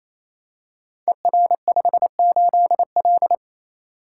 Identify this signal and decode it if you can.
Morse code, heard as a keyed tone, about 29 wpm, 700 Hz. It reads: ER58L
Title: Code Groups - 5 Letter:Numbers - 28wpm